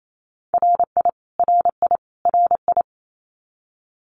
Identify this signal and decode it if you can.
Morse code, heard as a keyed tone, about 28 wpm, 700 Hz. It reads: RI RI RI
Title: US States Repeated 3x 28wpm